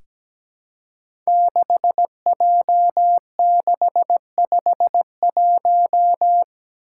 Morse code, heard as a keyed tone, about 17 wpm, 700 Hz. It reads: 6J651
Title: Five Challenging Letter-Number Combinations 17wpm